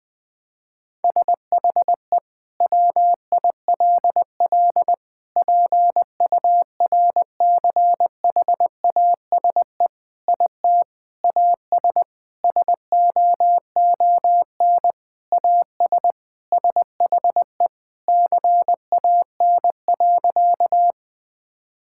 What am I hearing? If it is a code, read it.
Morse code, heard as a keyed tone, about 20 wpm, 700 Hz. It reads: SHE WILL PURCHASE IT AS SOON AS SHE CAN.
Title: Sentences from Top 1000 Words - Review Part 5 - 20wpm